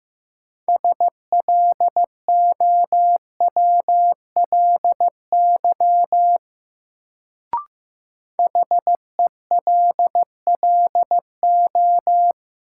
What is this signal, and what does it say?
Morse code, heard as a keyed tone, about 15 wpm, 700 Hz. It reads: SLOWLY HELLO
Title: Sentences from Top 1100 Words - New Word Drill - 15wpm